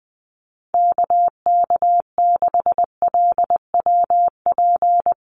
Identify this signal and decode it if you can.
Morse code, heard as a keyed tone, about 20 wpm, 700 Hz. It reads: KK6LWP
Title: USA Call Signs Repeated 2x 20wpm